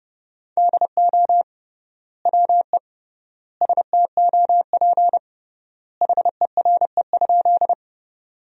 Morse code, heard as a keyed tone, about 30 wpm, 700 Hz. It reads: DO WE STOP HERE?
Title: 3x Word-Spacing - Sentences from Top 300 Words - 30wpm